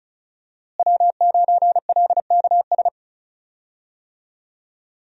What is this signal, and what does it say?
Morse code, heard as a keyed tone, about 34 wpm, 700 Hz. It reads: W9LKS
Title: USA Call Signs - 5 Characters - Repeated 2x - 35wpm